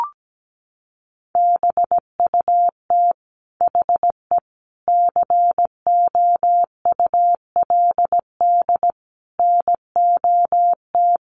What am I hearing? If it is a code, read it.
Morse code, heard as a keyed tone, about 17 wpm, 700 Hz. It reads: BUT HE COULD NOT
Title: Sentences from Top 600 Words 17wpm